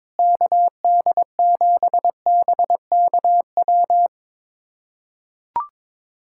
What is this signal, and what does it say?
Morse code, heard as a keyed tone, about 22 wpm, 700 Hz. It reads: KD7BKW
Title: USA Callsign DE Callsign - 6 Characters - 22wpm